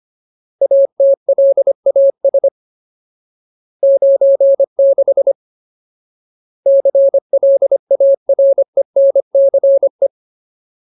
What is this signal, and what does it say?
Morse code, heard as a keyed tone, about 25 wpm, 550 Hz. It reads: ATLAS 96 CLARENCE
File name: Ultimate Word List / 25WPM Ultimate